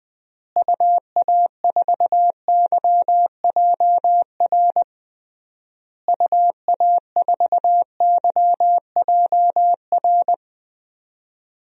Morse code, heard as a keyed tone, about 21 wpm, 700 Hz. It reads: UA4YJR UA4YJR
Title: DXCC-100 Call Signs - 6 Characters - Repeated 3x - 20wpm